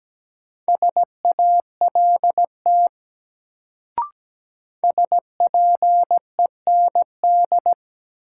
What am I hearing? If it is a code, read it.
Morse code, heard as a keyed tone, about 17 wpm, 700 Hz. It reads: SALT SPEND
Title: Sets of 5 Words - New Word Drill - 17wpm